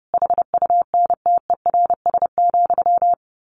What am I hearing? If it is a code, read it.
Morse code, heard as a keyed tone, about 30 wpm, 700 Hz. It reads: HUNTERS,
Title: A Sound Of Thunder 30wpm